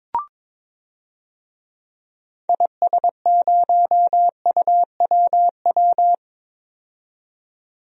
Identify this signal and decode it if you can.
Morse code, heard as a keyed tone, about 22 wpm, 700 Hz. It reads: IS0UWW
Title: DXCC-100 Call Signs - 6 Characters - 22wpm